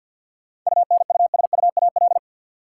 Morse code, heard as a keyed tone, about 52 wpm, 700 Hz. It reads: UNUSUAL
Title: Sentences from Top 1400 Words - New Word Drill - 50wpm